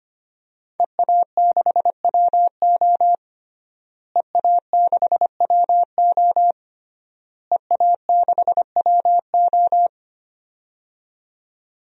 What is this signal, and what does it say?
Morse code, heard as a keyed tone, about 25 wpm, 700 Hz. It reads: EA6WO EA6WO EA6WO
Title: DXCC-100 Call Signs - 5 Characters - Repeated 3x - 25wpm